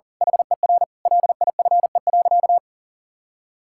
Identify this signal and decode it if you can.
Morse code, heard as a keyed tone, about 40 wpm, 700 Hz. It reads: HER LIFE.
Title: Sentences from Top 1000 Words - Review Part 6 - 40wpm